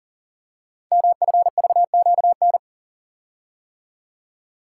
Morse code, heard as a keyed tone, about 40 wpm, 690 Hz. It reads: MFVQN
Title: Code Groups - 5 Letters - 40wpm